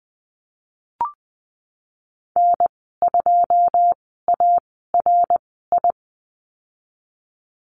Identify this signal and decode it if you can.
Morse code, heard as a keyed tone, about 20 wpm, spaced as for 15 wpm, 700 Hz. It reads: N2ARI
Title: 088 Lesson 15g - Number 5 - Call Signs 20wpm